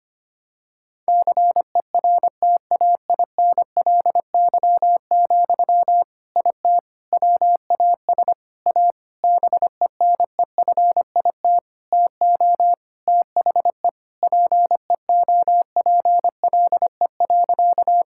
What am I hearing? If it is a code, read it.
Morse code, heard as a keyed tone, about 25 wpm, 700 Hz. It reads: CERTAINLY, IT WAS A BENEFIT TO THE PEOPLE.
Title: Sentences from Top 1000 Words - Review Part 6 - 25wpm